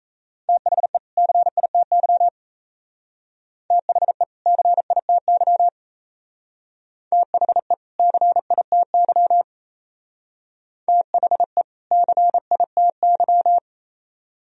Morse code, heard as a keyed tone, about 36 wpm, 700 Hz. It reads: THE CITY THE CITY THE CITY THE CITY